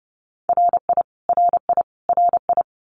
Morse code, heard as a keyed tone, about 30 wpm, 700 Hz. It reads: RI RI RI
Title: US States Repeated 3x 30wpm